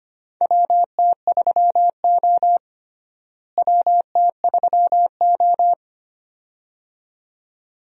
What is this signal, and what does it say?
Morse code, heard as a keyed tone, about 25 wpm, 700 Hz. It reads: WT3O WT3O
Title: USA Call Signs - 4 Characters - Repeated 2x - 25wpm